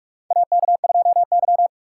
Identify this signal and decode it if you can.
Morse code, heard as a keyed tone, about 46 wpm, 690 Hz. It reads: AK2Y
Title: USA Call Signs - 4 Characters - 3x Character Spacing - 45wpm